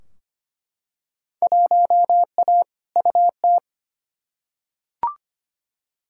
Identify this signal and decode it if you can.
Morse code, heard as a keyed tone, about 25 wpm, 700 Hz. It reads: 1A UT
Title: Field Day Exchanges - 25wpm